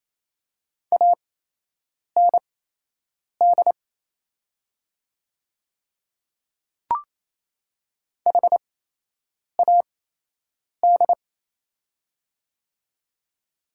Morse code, heard as a keyed tone, about 28 wpm, spaced as for 8 wpm, 690 Hz. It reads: AND HAD
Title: 8x Character Spacing - Top 100 Words - 28wpm